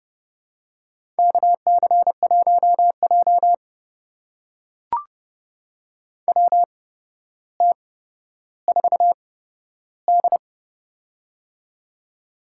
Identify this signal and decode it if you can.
Morse code, heard as a keyed tone, about 30 wpm, spaced as for 13 wpm, 700 Hz. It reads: KC1J WT4D
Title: USA Call Signs - 4 Characters - 8x Character Spacing - 30wpm